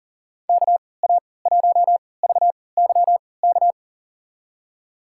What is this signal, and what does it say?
Morse code, heard as a keyed tone, about 40 wpm, spaced as for 23 wpm, 700 Hz. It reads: KA1VYK
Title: USA Call Signs - 6 Characters - 3x Character Spacing - 40wpm